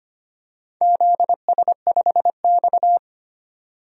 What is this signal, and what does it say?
Morse code, heard as a keyed tone, about 25 wpm, 700 Hz. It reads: ZS5X
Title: DXCC-100 Call Signs - 4 Characters - 25wpm